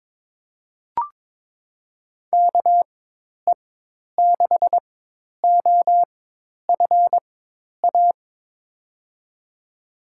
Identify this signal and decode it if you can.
Morse code, heard as a keyed tone, about 22 wpm, spaced as for 10 wpm, 700 Hz. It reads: KE6OFA
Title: USA Call Signs - 6 Characters - 4x Character Spacing - 22wpm